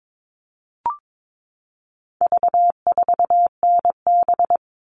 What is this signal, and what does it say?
Morse code, heard as a keyed tone, about 22 wpm, 700 Hz. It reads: V4NB
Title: All Cut Numbers - 4-Digit 22wpm